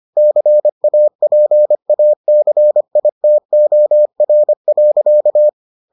Morse code, heard as a keyed tone, about 25 wpm, 600 Hz. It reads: CAPACITOR.